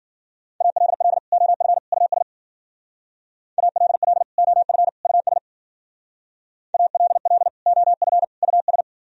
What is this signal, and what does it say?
Morse code, heard as a keyed tone, about 50 wpm, 700 Hz. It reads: ALL OF US ALL OF US ALL OF US
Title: Speed-Racing - Sets of 3 Words 40wpm